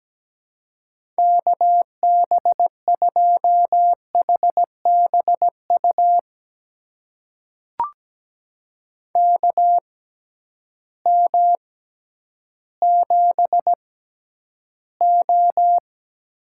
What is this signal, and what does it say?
Morse code, heard as a keyed tone, about 17 wpm, spaced as for 10 wpm, 700 Hz. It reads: KB2HBU KM7O
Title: USA Call Signs - 6x Character Spacing - 17wpm